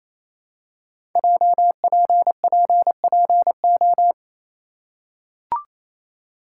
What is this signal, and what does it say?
Morse code, heard as a keyed tone, about 28 wpm, 700 Hz. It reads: JPPPO
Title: Code Groups - 5 Letters - 28wpm